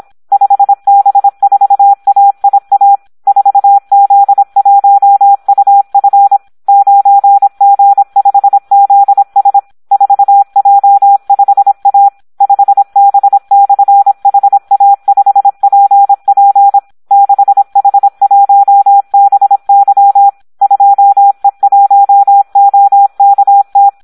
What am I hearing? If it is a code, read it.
Morse code, heard as a keyed tone, about 26 wpm, 800 Hz. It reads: 5B4AIA 4Z1UF 9G5ZS 4J5A 5B/HA5PP 6H1BY 2E1OKT
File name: Audio practica CW / SD 1 26